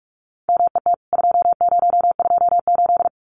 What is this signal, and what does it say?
Morse code, heard as a keyed tone, about 44 wpm, 700 Hz. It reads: MET 2028
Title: CWT Exchange 45wpm